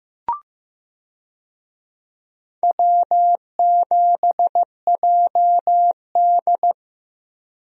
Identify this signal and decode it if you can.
Morse code, heard as a keyed tone, about 15 wpm, 700 Hz. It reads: W7JD
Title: USA Call Signs - 4 Characters - Repeated 3x - 15wpm